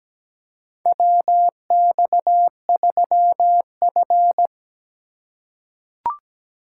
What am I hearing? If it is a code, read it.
Morse code, heard as a keyed tone, about 17 wpm, 700 Hz. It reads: WX3F
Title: USA Call Signs - 4 Characters - 2x Character Spacing - 17wpm